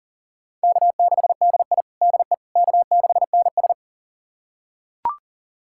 Morse code, heard as a keyed tone, about 40 wpm, 700 Hz. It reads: K6DI DE K6NS
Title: USA Callsign DE Callsign - 4 Characters - 40wpm